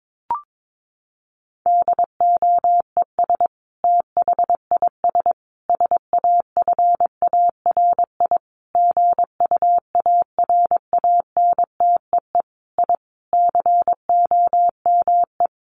Morse code, heard as a keyed tone, about 22 wpm, 700 Hz. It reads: DOES THIS SAFARI GUARANTEE I COME
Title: A Sound Of Thunder 22wpm